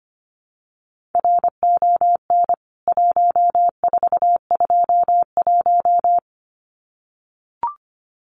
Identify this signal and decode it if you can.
Morse code, heard as a keyed tone, about 25 wpm, 700 Hz. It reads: RON 1421
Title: CWT Exchange 25wpm